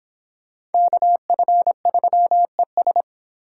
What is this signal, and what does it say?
Morse code, heard as a keyed tone, about 30 wpm, 700 Hz. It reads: KF3ES